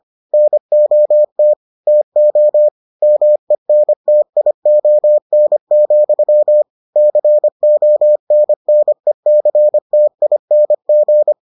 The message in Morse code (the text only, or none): NOT TO MENTION, CONNECTING